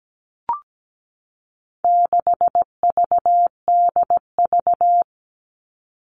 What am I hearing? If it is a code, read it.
Morse code, heard as a keyed tone, about 17 wpm, 700 Hz. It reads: 6VDV
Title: All Cut Numbers - 4-Digit 17wpm